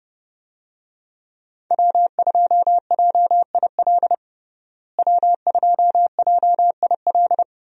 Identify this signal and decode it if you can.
Morse code, heard as a keyed tone, about 31 wpm, 700 Hz. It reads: W2JIL W2JIL
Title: USA Call Signs - 5 Characters - Repeated 3x - 30wpm